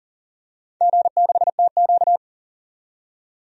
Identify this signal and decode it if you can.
Morse code, heard as a keyed tone, about 40 wpm, 700 Hz. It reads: G6TQ
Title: DXCC-100 Call Signs - 4 Characters - 40wpm